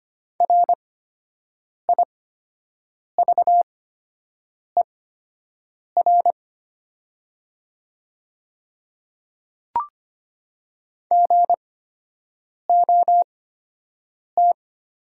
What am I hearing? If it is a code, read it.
Morse code, heard as a keyed tone, about 25 wpm, spaced as for 7 wpm, 700 Hz. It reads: RIVER GOT
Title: 8x Character Spacing - Top 300 Words - 25wpm